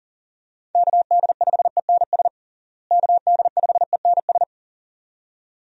Morse code, heard as a keyed tone, about 40 wpm, 700 Hz. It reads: KD5ENS KD5ENS
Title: USA Call Signs - 6 Characters - Repeated 2x - 40wpm